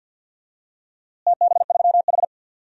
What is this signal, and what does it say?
Morse code, heard as a keyed tone, about 53 wpm, 700 Hz. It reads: TB3H
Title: DXCC-100 Call Signs - 4 Characters - Repeated 3x - 50wpm